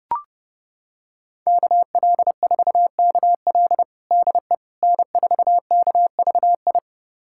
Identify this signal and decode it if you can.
Morse code, heard as a keyed tone, about 32 wpm, 700 Hz. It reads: KL4KL DE N4KVI
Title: USA Callsign DE Callsign - 5 Characters - 30wpm